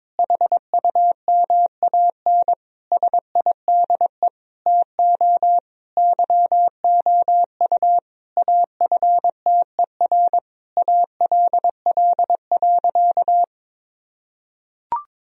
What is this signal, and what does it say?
Morse code, heard as a keyed tone, about 22 wpm, 700 Hz. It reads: HUMAN SIDE TO YOU AFTER ALL.
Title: Sentences from Top 1000 Words - Review Part 3 - 22wpm